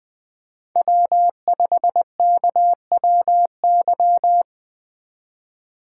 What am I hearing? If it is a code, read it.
Morse code, heard as a keyed tone, about 23 wpm, 700 Hz. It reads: W5KWY